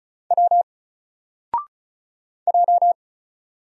Morse code, heard as a keyed tone, about 36 wpm, 700 Hz. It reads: W J
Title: Single Letter-Number Alt 35wpm